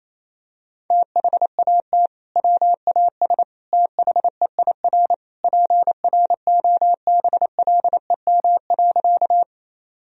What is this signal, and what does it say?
Morse code, heard as a keyed tone, about 28 wpm, 700 Hz. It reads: THAT WAS THEIR PROBLEM.
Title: Sentences from Top 1000 Words - Review Part 2 - 28wpm